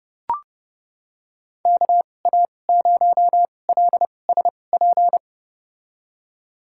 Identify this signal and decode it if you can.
Morse code, heard as a keyed tone, about 32 wpm, spaced as for 22 wpm, 700 Hz. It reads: KA0LSP
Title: USA Call Signs - 6 Characters - 2x Character Spacing - 30wpm